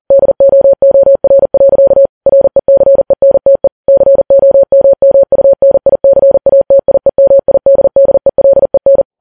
Morse code, heard as a keyed tone, about 40 wpm, 550 Hz. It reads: DOOR. RECENTE COMMUNICATIEMIDDELEN
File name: Files Web tls / Morse on9fv / morse40kps